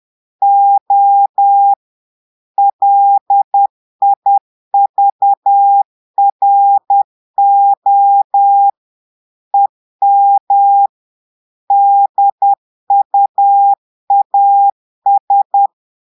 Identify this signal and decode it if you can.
Morse code, heard as a keyed tone, about 10 wpm, 800 Hz. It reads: O LIVRO EM DUAS
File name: Portugais - texte en clair 1 - vitesse - 10 mots minute